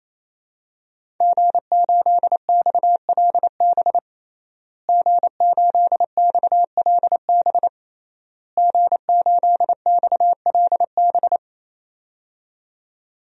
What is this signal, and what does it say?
Morse code, heard as a keyed tone, about 28 wpm, 700 Hz. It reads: G8XLB G8XLB G8XLB
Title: DXCC-100 Call Signs - 5 Characters - Repeated 3x - 28wpm